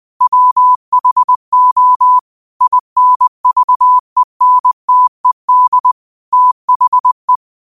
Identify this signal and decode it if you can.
Morse code, heard as a keyed tone, about 20 wpm, 1000 Hz. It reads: WHO INVENTED THE
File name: qotdfiles / 20wpm